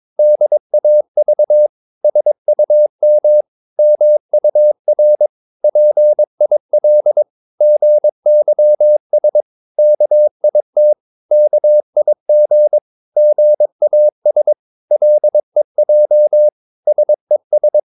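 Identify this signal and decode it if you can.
Morse code, heard as a keyed tone, about 22 wpm, 600 Hz. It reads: DAV SUM MUR PIL GYS KIT KIG GAS LEJ SES
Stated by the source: Morse practice of short Danish words at 22 wpm, from CW med Gnister.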